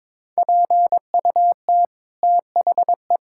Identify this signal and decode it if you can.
Morse code, heard as a keyed tone, about 22 wpm, 700 Hz. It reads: PUT THE
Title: Sentences from Top 1600 Words 22wpm